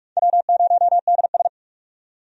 Morse code, heard as a keyed tone, about 46 wpm, 700 Hz. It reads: W0DS
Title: USA Call Signs - 4 Characters - 45wpm